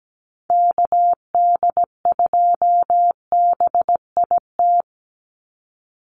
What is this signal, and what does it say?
Morse code, heard as a keyed tone, about 17 wpm, 700 Hz. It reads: KD2BIT
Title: USA Call Signs 17wpm